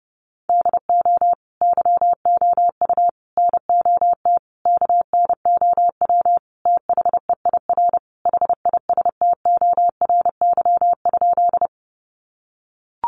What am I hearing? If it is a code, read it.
Morse code, heard as a keyed tone, about 30 wpm, 700 Hz. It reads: DO YOU NOT KNOW THEIR HISTORY?
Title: Sentences from Top 600 Words 30wpm